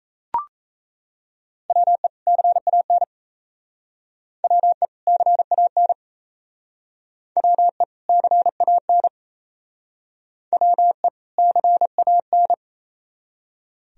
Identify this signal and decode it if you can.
Morse code, heard as a keyed tone, about 37 wpm, 700 Hz. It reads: WE CAN WE CAN WE CAN WE CAN